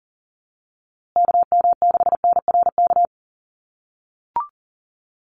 Morse code, heard as a keyed tone, about 40 wpm, 700 Hz. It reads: KM6NRK
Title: USA Call Signs Repeated 2x 40wpm